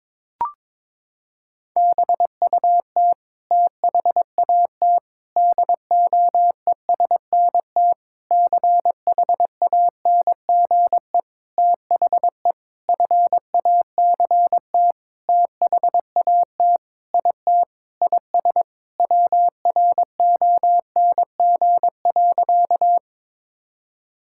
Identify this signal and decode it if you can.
Morse code, heard as a keyed tone, about 22 wpm, 700 Hz. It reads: BUT THAT DOESNT CHANGE THE FACT THAT IT IS WRONG.
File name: Sentences with 7-Word Phrases - 22wpm